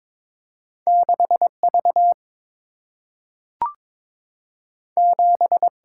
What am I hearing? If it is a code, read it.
Morse code, heard as a keyed tone, about 22 wpm, 700 Hz. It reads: 6 V 7
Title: Single-Letter Single-Number 22wpm